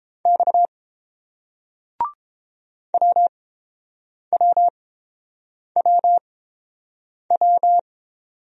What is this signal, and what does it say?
Morse code, heard as a keyed tone, about 29 wpm, spaced as for 7 wpm, 700 Hz. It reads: X WWWW